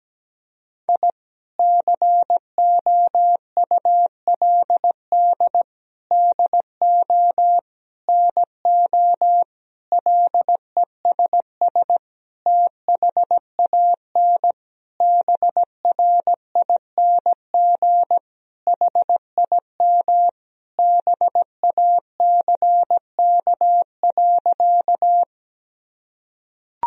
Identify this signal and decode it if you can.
Morse code, heard as a keyed tone, about 17 wpm, 700 Hz. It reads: I COULD DO NO LESS THAN BRING HIM BACK.
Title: Sentences from Top 1000 Words - Review Part 3 - 17wpm